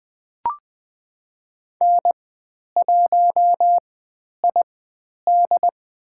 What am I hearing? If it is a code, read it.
Morse code, heard as a keyed tone, about 20 wpm, spaced as for 10 wpm, 700 Hz. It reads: N1ID
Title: 109 Lesson 18g - Letter m - Call Signs 20wpm